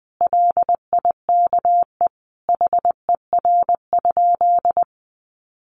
Morse code, heard as a keyed tone, about 20 wpm, 700 Hz. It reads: LIKE HER?
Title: Sentences from Top 100 Words 20wpm